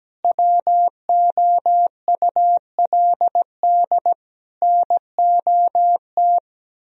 Morse code, heard as a keyed tone, about 17 wpm, 700 Hz. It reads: WOULD NOT
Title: Sentences from Top 1000 Words - Review Part 4 - 17wpm